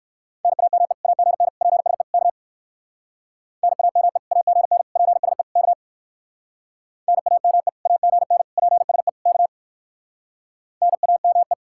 Morse code, heard as a keyed tone, about 56 wpm, 700 Hz. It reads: NAME AGN PSE K NAME AGN PSE K NAME AGN PSE K NAME